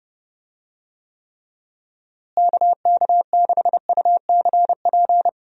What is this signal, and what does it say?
Morse code, heard as a keyed tone, about 32 wpm, 700 Hz. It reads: KK6UCP
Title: USA Call Signs - 6 Characters - Repeated 2x - 30wpm